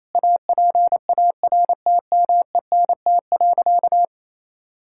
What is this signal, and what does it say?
Morse code, heard as a keyed tone, about 28 wpm, 700 Hz. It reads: APARTMENT.
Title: Sentences from Top 1000 Words - Review Part 6 - 28wpm